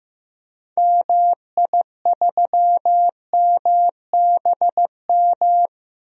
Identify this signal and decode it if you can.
Morse code, heard as a keyed tone, about 15 wpm, 700 Hz. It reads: MI3MBM
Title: DXCC-100 Call Signs - 6 Characters - 15wpm